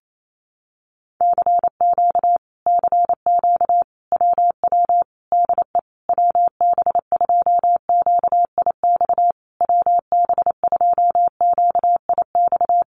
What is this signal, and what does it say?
Morse code, heard as a keyed tone, about 28 wpm, 700 Hz. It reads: CQ CQ WW DE WB2QIX WB2QIX
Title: Common QSO Phrases Encore 28wpm